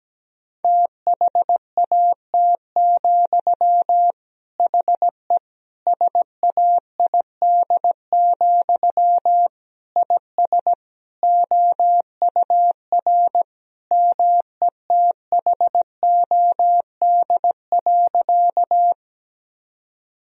Morse code, heard as a keyed tone, about 17 wpm, 700 Hz. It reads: THAT, HE SAID, IS OUR METHOD.
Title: Sentences from Top 1000 Words - Review Part 3 - 17wpm